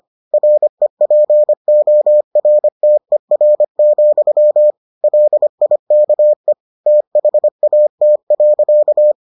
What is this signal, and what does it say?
Morse code, heard as a keyed tone, about 25 wpm, 600 Hz. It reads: REPORTER, LIKE THAT.